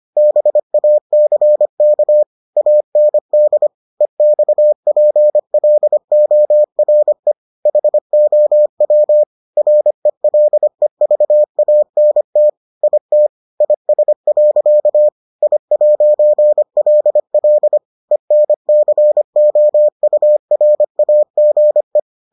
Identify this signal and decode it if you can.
Morse code, heard as a keyed tone, about 25 wpm, 600 Hz. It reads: BACK AND EXPLORE HOW RELEVANT IT IS. I'LL ENCOURAGE